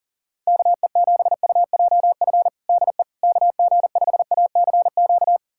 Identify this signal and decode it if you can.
Morse code, heard as a keyed tone, about 37 wpm, 700 Hz. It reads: KE7UJF DE KG5ACQ
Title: USA Callsign DE Callsign - 6 Characters - 40wpm